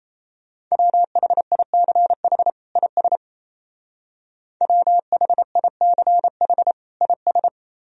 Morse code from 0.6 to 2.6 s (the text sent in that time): WHICH